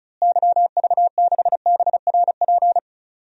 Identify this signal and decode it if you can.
Morse code, heard as a keyed tone, about 34 wpm, 700 Hz. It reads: YV6BRP
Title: DXCC-100 Call Signs - 6 Characters - Repeated 2x - 35wpm